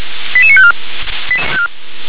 Five-tone selective call (ZVEI), heard as quick runs of five tones, tones of about 70 ms. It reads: 88954